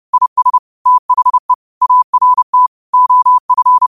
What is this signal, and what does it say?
Morse code, heard as a keyed tone, about 30 wpm, 1000 Hz. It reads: IS THE ART OF